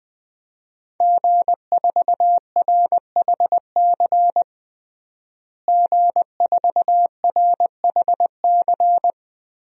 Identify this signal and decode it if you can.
Morse code, heard as a keyed tone, about 21 wpm, 700 Hz. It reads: G4RHC G4RHC
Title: DXCC-100 Call Signs - 5 Characters - Repeated 3x - 20wpm